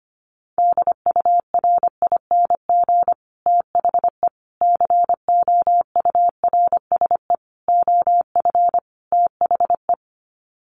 Morse code, heard as a keyed tone, about 25 wpm, 700 Hz. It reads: DURING THE COURSE OF THE
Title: Sets of 5 Words Encore 25wpm